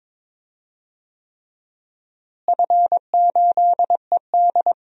Morse code, heard as a keyed tone, about 22 wpm, 700 Hz. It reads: F8ED
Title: DXCC-100 Call Signs - 4 Characters - Repeated 3x - 22wpm